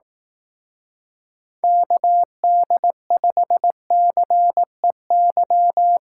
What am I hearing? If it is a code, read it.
Morse code, heard as a keyed tone, about 18 wpm, 700 Hz. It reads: KD5CEY
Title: Speed Racing - USA 2x3 Call Signs 15wpm